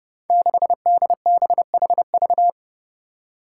Morse code, heard as a keyed tone, about 30 wpm, 700 Hz. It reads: 6DBHV
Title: Five Challenging Letter-Number Combinations 30wpm